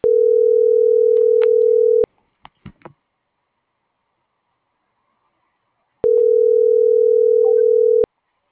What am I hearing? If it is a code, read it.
Ringback tone (North America), 440 + 480 Hz.